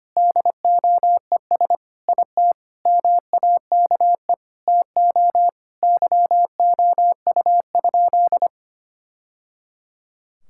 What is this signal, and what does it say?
Morse code, heard as a keyed tone, about 25 wpm, 700 Hz. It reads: DOES IT MAKE TO YOU?
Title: Sentences from Top 1000 Words - Review Part 5 - 25wpm